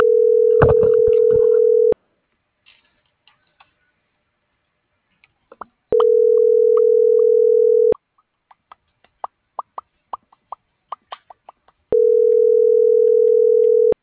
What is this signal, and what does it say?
Ringback tone (North America), 440 + 480 Hz.